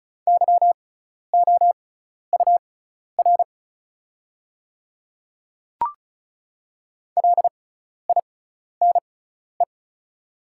Morse code, heard as a keyed tone, about 36 wpm, spaced as for 12 wpm, 700 Hz. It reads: YOUR LINE
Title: 6x Character Spacing - Top 200 Words - 35wpm